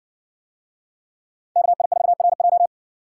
Morse code, heard as a keyed tone, about 57 wpm, 700 Hz. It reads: KI4RJ